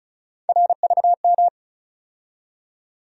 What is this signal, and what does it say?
Morse code, heard as a keyed tone, about 36 wpm, 700 Hz. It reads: RVM
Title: USA Call Sign Postfixes - 3 Letters 35wpm